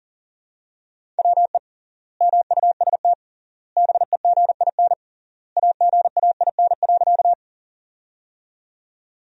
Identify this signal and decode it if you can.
Morse code, heard as a keyed tone, about 37 wpm, 700 Hz. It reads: WE MUST BEGIN AGAIN.
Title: 3x Word-Spacing - Sentences from Top 300 Words - 40wpm